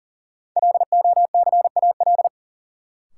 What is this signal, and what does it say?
Morse code, heard as a keyed tone, about 40 wpm, 700 Hz. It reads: LOCAL
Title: Sentences from Top 1500 Words - New Word Drill Alt - 40wpm